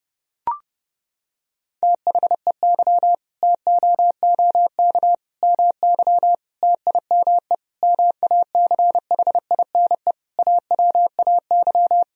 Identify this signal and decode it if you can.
Morse code, heard as a keyed tone, about 32 wpm, 700 Hz. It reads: THEY TOOK MY TIME MACHINE AWAY
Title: Sentences from Top 1000 Words - Review Part 2 - 30wpm